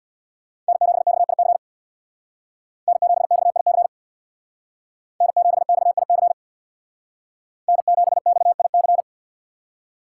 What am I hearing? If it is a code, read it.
Morse code, heard as a keyed tone, about 64 wpm, 700 Hz. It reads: N7XIC N7XIC N7XIC N7XIC